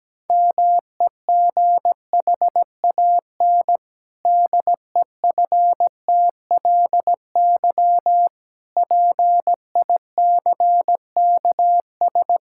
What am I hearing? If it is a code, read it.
Morse code, heard as a keyed tone, about 17 wpm, 700 Hz. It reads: MEGHAN DEFTLY PICKS
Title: Pangram Challenge v2 - 17wpm